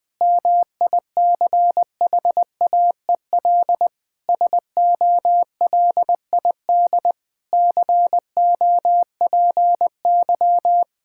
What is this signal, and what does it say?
Morse code, heard as a keyed tone, about 20 wpm, 700 Hz. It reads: MICHAEL SOLID COPY
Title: K1USN SST QSO Practice 20wpm